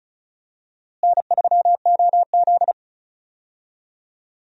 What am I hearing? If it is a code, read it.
Morse code, heard as a keyed tone, about 36 wpm, 700 Hz. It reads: N3OZ
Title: USA Call Signs - 4 Characters - Repeated 2x - 35wpm